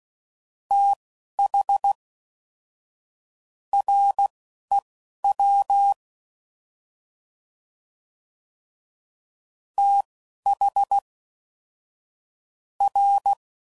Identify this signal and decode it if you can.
Morse code, heard as a keyed tone, about 16 wpm, spaced as for 11 wpm, 800 Hz. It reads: TH REW TH R
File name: Prefixes / FLs & THs / Fast TH Prefixes